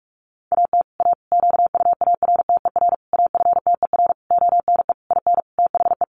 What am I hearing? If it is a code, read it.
Morse code, heard as a keyed tone, about 44 wpm, 700 Hz. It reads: AT A QUARTER AFTER ONE IN THE
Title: Sentences from Top 1300 Words 45wpm